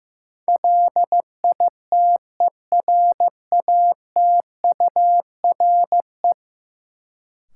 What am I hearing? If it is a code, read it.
Morse code, heard as a keyed tone, about 15 wpm, 700 Hz. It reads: LITERATURE
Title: Sentences from Top 600 Words - New Word Drill - 15wpm